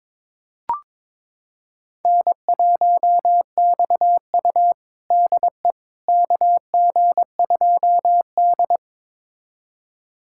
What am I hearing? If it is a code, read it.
Morse code, heard as a keyed tone, about 21 wpm, 700 Hz. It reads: N1XU DE KG2D
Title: USA Callsign DE Callsign - 4 Characters - 22wpm